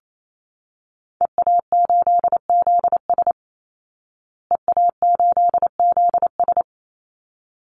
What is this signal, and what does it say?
Morse code, heard as a keyed tone, about 28 wpm, 700 Hz. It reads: EA8ZS EA8ZS
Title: Call Signs Repeated 3x - 28wpm